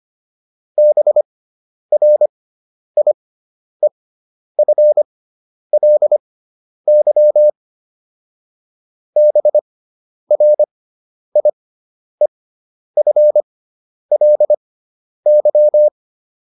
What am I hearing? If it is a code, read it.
Morse code, heard as a keyed tone, about 25 wpm, spaced as for 10 wpm, 600 Hz. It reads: BRIEFLY BRIEFLY
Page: fundamental-practice-files